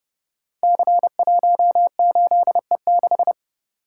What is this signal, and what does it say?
Morse code, heard as a keyed tone, about 29 wpm, 700 Hz. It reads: C18E6
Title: Code Groups - 5 Letter:Numbers - 30wpm